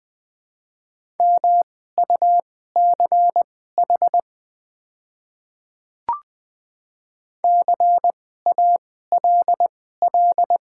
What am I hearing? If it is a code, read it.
Morse code, heard as a keyed tone, about 20 wpm, spaced as for 15 wpm, 700 Hz. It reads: MUCH CALL
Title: 2x Character Spacing - Top 200 Words - 20wpm